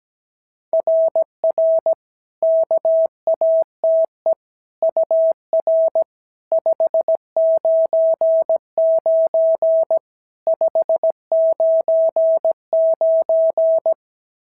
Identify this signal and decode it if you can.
Morse code, heard as a keyed tone, about 18 wpm, 650 Hz. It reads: RR KATE UR 599 599
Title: SOTA QSO Practice 17wpm